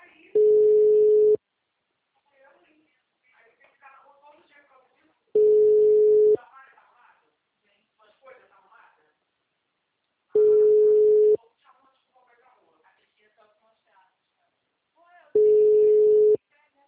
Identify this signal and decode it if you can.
Ringback tone (Europe, 425 Hz), with speech.
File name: out- -2003- - - .71299